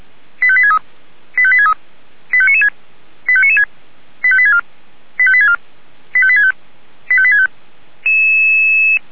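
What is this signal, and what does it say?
Five-tone selective call (ZVEI), heard as quick runs of five tones, tones of about 70 ms. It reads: 85863 85863 85996 85996 85864 85864 85865 85865
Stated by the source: ZVEI five-tone call received by SDR.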